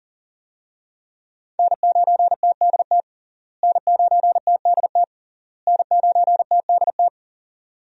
Morse code, heard as a keyed tone, about 40 wpm, 700 Hz. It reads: N9TDT N9TDT N9TDT
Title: USA Call Signs - 5 Characters - Repeated 3x - 40wpm